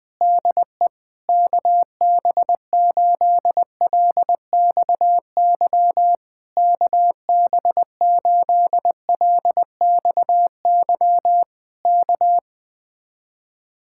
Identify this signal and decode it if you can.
Morse code, heard as a keyed tone, about 20 wpm, 700 Hz. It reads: DE KB8LXY KB8LXY K
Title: Speed-Racing Common QSO Phrases Encore - 17wpm